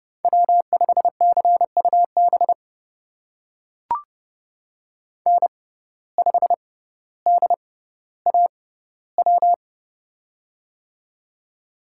Morse code, heard as a keyed tone, about 31 wpm, spaced as for 16 wpm, 700 Hz. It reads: W5CUB N5DAW
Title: USA Call Signs - 5 Characters - 6x Character Spacing - 30wpm